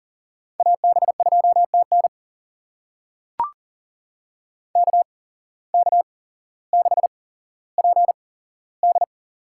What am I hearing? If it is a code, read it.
Morse code, heard as a keyed tone, about 40 wpm, spaced as for 17 wpm, 700 Hz. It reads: AB2TN KK6PD
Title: USA Call Signs - 5 Characters - 8x Character Spacing - 40wpm